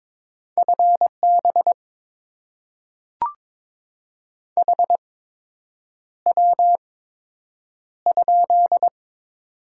Morse code, heard as a keyed tone, about 22 wpm, spaced as for 8 wpm, 700 Hz. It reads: FB HW?
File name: Top 20 QSO Elements - 8x Character Spacing - 22wpm